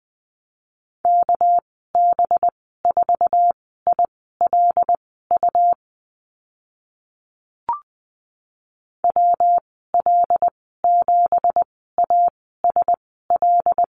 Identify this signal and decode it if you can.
Morse code, heard as a keyed tone, about 20 wpm, spaced as for 14 wpm, 700 Hz. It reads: KB4ILU WL7ASL
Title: 207 Lesson 33g - Letter j - Call Signs 20wpm